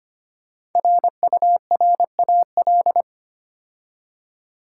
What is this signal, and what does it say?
Morse code, heard as a keyed tone, about 25 wpm, 700 Hz. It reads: RURAL
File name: 5 Letter Words - First 3 Letters Given - 25wpm